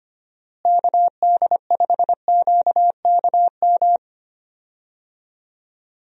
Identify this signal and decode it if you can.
Morse code, heard as a keyed tone, about 25 wpm, 700 Hz. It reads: KD5QKM
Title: USA Call Signs - 6 Characters - Repeated 3x - 25wpm